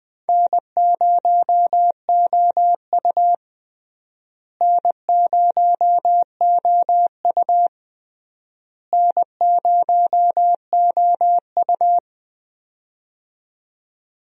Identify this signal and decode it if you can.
Morse code, heard as a keyed tone, about 20 wpm, 700 Hz. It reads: N0OU N0OU N0OU
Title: USA Call Signs - 4 Characters - Repeated 3x - 20wpm